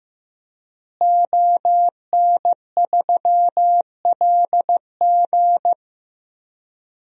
Morse code, heard as a keyed tone, about 15 wpm, 700 Hz. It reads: ON3LG
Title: DXCC-100 Call Signs - 5 Characters - Repeated 2x - 15wpm